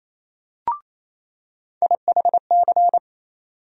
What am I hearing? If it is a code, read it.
Morse code, heard as a keyed tone, about 28 wpm, 700 Hz. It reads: IHC
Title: USA Call Sign Postfixes - 3 Letters 28wpm